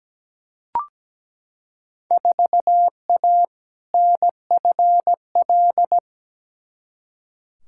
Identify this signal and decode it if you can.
Morse code, heard as a keyed tone, about 17 wpm, 700 Hz. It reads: 4A NFL
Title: Field Day Exchanges - 17wpm